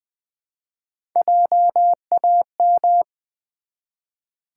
Morse code, heard as a keyed tone, about 20 wpm, 700 Hz. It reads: JAM
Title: Build A Word - 1 Letter At A Time - 3-Letter Words - 20wpm